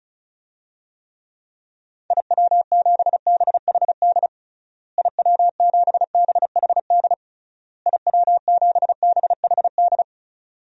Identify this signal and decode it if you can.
Morse code, heard as a keyed tone, about 36 wpm, 700 Hz. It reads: IW7BHD IW7BHD IW7BHD
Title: DXCC-100 Call Signs - 6 Characters - Repeated 3x - 35wpm